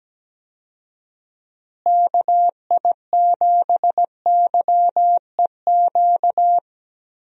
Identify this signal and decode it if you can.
Morse code, heard as a keyed tone, about 17 wpm, 700 Hz. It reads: KI7YEQ
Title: USA Call Signs - 6 Characters - Repeated 3x - 17wpm